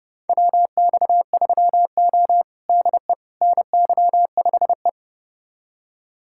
Morse code, heard as a keyed tone, about 32 wpm, 700 Hz. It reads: WX3O DE NY5E
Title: USA Callsign DE Callsign - 4 Characters - 30wpm